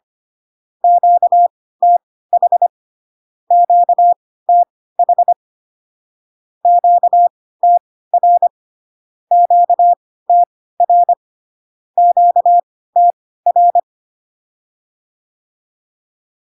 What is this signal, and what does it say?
Morse code, heard as a keyed tone, about 25 wpm, spaced as for 16 wpm, 700 Hz. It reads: QTH QTH QTR QTR QTR
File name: cw / Q-codes most common repeated 3 times 10 25wpm ik1hge